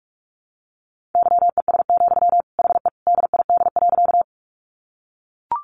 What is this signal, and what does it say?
Morse code, heard as a keyed tone, about 44 wpm, 700 Hz. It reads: YES, HE DID.
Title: Sentences from Top 500 Words 45wpm